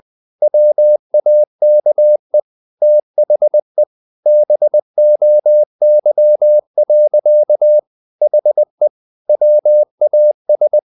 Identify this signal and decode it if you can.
Morse code, heard as a keyed tone, about 20 wpm, 600 Hz. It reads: WAKE THE BOY. HE WAS
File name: the old man and the sea / the old man and the sea part 1 20wpm